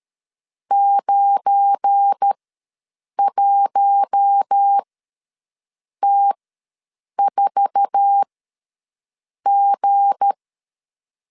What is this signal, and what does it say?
Morse code, heard as a keyed tone, about 13 wpm, spaced as for 6 wpm, 800 Hz. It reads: 91T4G
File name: LEZIONI / 9 28 ws 12wpm 3354